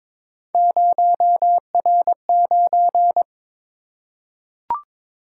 Morse code, heard as a keyed tone, about 22 wpm, 700 Hz. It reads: 0R9
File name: practice / 22wpm / Three Letter-Number Combinations - 22wpm